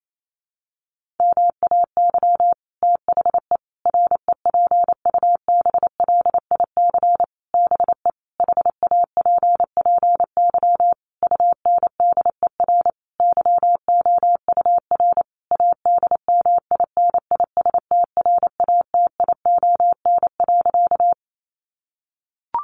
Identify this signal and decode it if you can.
Morse code, heard as a keyed tone, about 28 wpm, 700 Hz. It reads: MAY THE REPUBLIC BE HAPPY UNDER YOUR ADMINISTRATION.
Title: Sentences from Top 1500 Words 28wpm